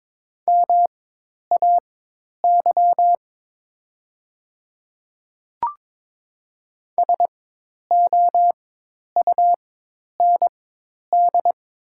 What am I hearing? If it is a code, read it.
Morse code, heard as a keyed tone, about 22 wpm, spaced as for 10 wpm, 700 Hz. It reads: MAY SOUND
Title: 4x Character Spacing - Top 100 Words - 22wpm